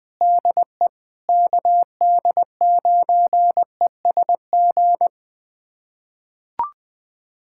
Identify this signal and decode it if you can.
Morse code, heard as a keyed tone, about 20 wpm, 700 Hz. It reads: DE KD9ESG
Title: USA Callsign DE Callsign 20wpm